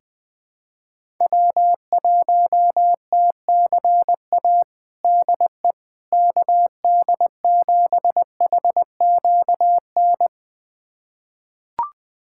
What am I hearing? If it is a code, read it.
Morse code, heard as a keyed tone, about 21 wpm, 700 Hz. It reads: W1TCA DE KD7HQN
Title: USA Callsign DE Callsign 20wpm